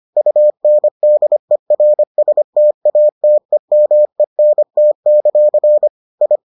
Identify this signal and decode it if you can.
Morse code, heard as a keyed tone, about 25 wpm, 600 Hz. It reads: UNDERSTATEMENT; I